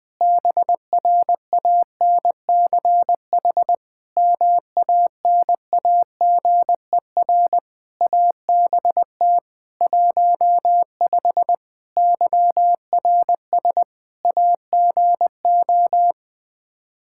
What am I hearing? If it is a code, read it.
Morse code, heard as a keyed tone, about 20 wpm, 700 Hz. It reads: BRANCH MANAGER ABT 15 YRS AGO
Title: Practice QSO - Exchange 3 - More Info - 20wpm